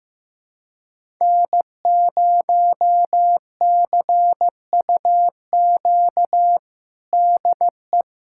Morse code, heard as a keyed tone, about 15 wpm, 700 Hz. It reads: N0CUQ DE
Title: USA Callsign DE Callsign - 5 Characters - 15wpm